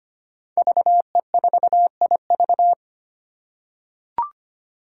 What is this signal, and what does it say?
Morse code, heard as a keyed tone, about 25 wpm, 700 Hz. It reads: VE4IV
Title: Five Challenging Letter-Number Combinations 25wpm